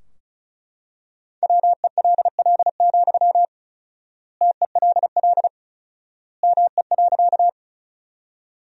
Morse code, heard as a keyed tone, about 34 wpm, 700 Hz. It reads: WELL, TELL ME.
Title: 4x Word-Spacing - Sentences from Top 200 Words - 35wpm